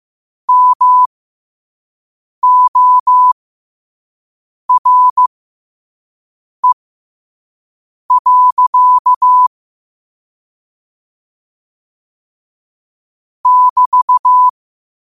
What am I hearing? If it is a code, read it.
Morse code, heard as a keyed tone, about 15 wpm, spaced as for 5 wpm, 1000 Hz. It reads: MORE. <BT>
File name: qotdfiles / 5wpm